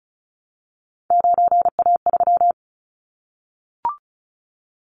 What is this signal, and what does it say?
Morse code, heard as a keyed tone, about 36 wpm, 700 Hz. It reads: 9A3
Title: Callsign Prefixes for Top 50 DXCC - 35wpm